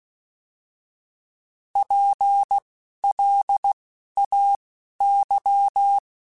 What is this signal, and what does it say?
Morse code, heard as a keyed tone, about 16 wpm, spaced as for 11 wpm, 800 Hz. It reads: PLAY
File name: General / General-mp3s / Houses @10